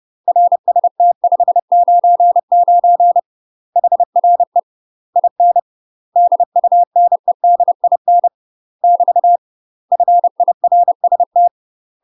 Morse code, heard as a keyed tone, about 30 wpm, 700 Hz. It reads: RST 599 HRE IN DUNEDIN <BT> FIRST
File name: UK MASTER QSOs TRAINING / 30WPM / UK MASTER QSOs 0021